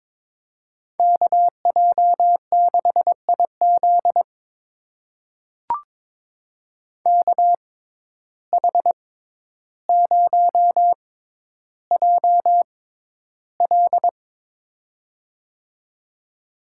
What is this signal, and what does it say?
Morse code, heard as a keyed tone, about 22 wpm, spaced as for 11 wpm, 700 Hz. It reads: KJ6IZ KH0JL
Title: USA Call Signs - 5 Characters - 6x Character Spacing - 22wpm